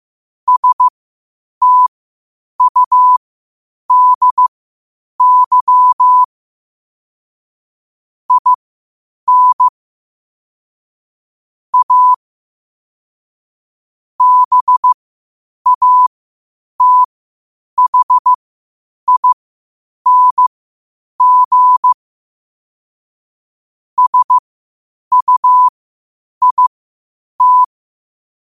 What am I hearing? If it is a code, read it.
Morse code, heard as a keyed tone, about 15 wpm, spaced as for 8 wpm, 1000 Hz. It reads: STUDY IN A BATHING SUIT
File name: qotdfiles / 8wpm